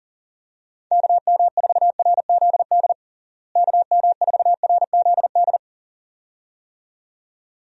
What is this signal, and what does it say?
Morse code, heard as a keyed tone, about 40 wpm, 700 Hz. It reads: KM4RZD KM4RZD
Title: USA Call Signs - 6 Characters - Repeated 2x - 40wpm